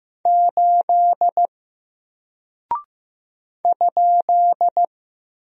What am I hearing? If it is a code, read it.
Morse code, heard as a keyed tone, about 15 wpm, 700 Hz. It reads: 8 ?
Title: Single Character (Letter:Number:Punctuation:Prosign) 15wpm